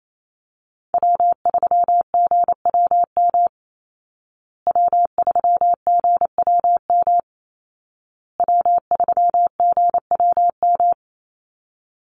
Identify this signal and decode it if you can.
Morse code, heard as a keyed tone, about 28 wpm, 700 Hz. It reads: W3GWM W3GWM W3GWM
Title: Call Signs Repeated 2x - 28wpm